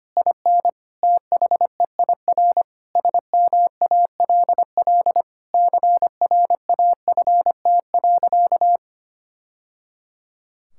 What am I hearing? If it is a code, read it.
Morse code, heard as a keyed tone, about 25 wpm, 700 Hz. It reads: IN THEIR SMALL CRAFT.
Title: Sentences from Top 1000 Words - Review Part 6 - 25wpm